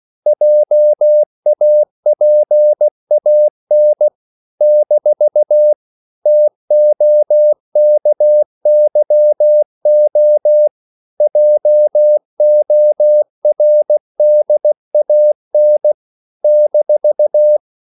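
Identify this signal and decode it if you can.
Morse code, heard as a keyed tone, about 16 wpm, 600 Hz. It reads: JAPAN - TOKYO JORDAN -
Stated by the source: CW practice of national capitals at 16 words per minute, from ON6ZQ.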